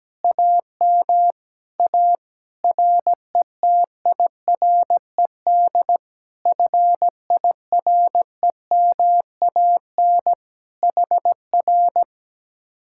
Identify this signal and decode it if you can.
Morse code, heard as a keyed tone, about 17 wpm, 700 Hz. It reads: AM A RETIRED FIREMAN HR
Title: Speed-Racing Common QSO Phrases - 17wpm